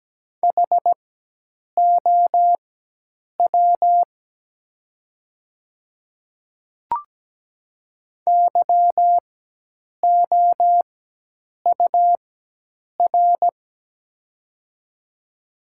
Morse code, heard as a keyed tone, about 17 wpm, spaced as for 8 wpm, 700 Hz. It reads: HOW YOUR
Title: 4x Character Spacing - Top 100 Words - 17wpm